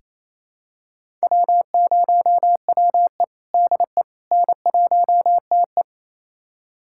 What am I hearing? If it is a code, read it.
Morse code, heard as a keyed tone, about 28 wpm, 700 Hz. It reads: W0WE DE N1TE
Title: USA Callsign DE Callsign - 4 Characters - 28wpm